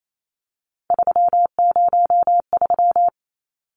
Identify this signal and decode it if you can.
Morse code, heard as a keyed tone, about 28 wpm, 700 Hz. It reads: 303
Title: Numbers - 3-Digit 28wpm